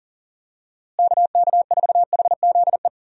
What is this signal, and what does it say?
Morse code, heard as a keyed tone, about 40 wpm, 700 Hz. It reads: KK4HZE
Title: USA Call Signs - 6 Characters - Repeated 2x - 40wpm